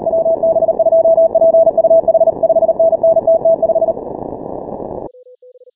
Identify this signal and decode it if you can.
Morse code, heard as a keyed tone, about 38 wpm, 660 Hz. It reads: WB2PUH 5NN TT5
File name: final / W1UJ-168-140-MorseRunner